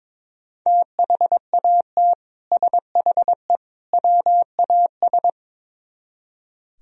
Morse code, heard as a keyed tone, about 24 wpm, 700 Hz. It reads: THAT SHE WAS
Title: Speed-Racing - Sets of 3 Words 22wpm